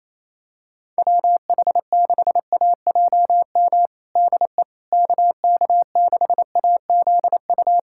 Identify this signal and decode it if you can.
Morse code, heard as a keyed tone, about 28 wpm, 700 Hz. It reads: WH6AJM DE KK6AZU
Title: USA Callsign DE Callsign 28wpm